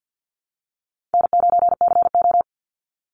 Morse code, heard as a keyed tone, about 50 wpm, 700 Hz. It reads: N9CO
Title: Easy Call Signs Repeated 2x - 50wpm